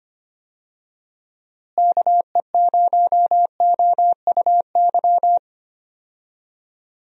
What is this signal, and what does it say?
Morse code, heard as a keyed tone, about 25 wpm, 700 Hz. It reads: KE0OUY
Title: USA Call Signs - 6 Characters - Repeated 3x - 25wpm